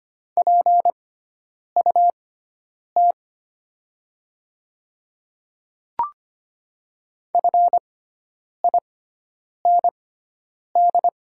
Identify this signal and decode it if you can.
Morse code, heard as a keyed tone, about 25 wpm, spaced as for 9 wpm, 700 Hz. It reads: PUT FIND
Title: 6x Character Spacing - Top 200 Words - 25wpm